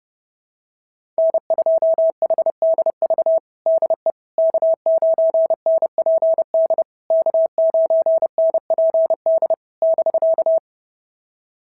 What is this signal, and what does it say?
Morse code, heard as a keyed tone, about 32 wpm, 650 Hz. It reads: N2HDV DE K9NPD K9NPD <BK>
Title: SOTA QSO Practice 30wpm